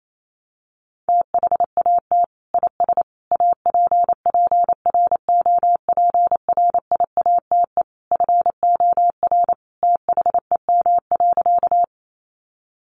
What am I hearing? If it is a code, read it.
Morse code, heard as a keyed tone, about 28 wpm, 700 Hz. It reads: THAT IS APPROPRIATE FOR THEM.
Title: Sentences from Top 900 Words 28wpm